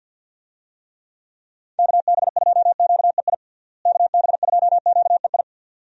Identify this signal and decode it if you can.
Morse code, heard as a keyed tone, about 51 wpm, 700 Hz. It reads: KB2QEI KB2QEI
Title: USA Call Signs - 6 Characters - Repeated 2x - 50wpm